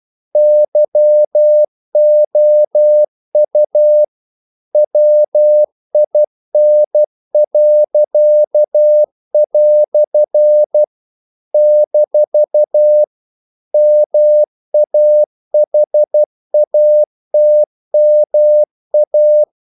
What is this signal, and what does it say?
Morse code, heard as a keyed tone, about 12 wpm, 600 Hz. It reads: YOU WIN." - MAHATMA
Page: English quotations in CW (telegraphy - morse)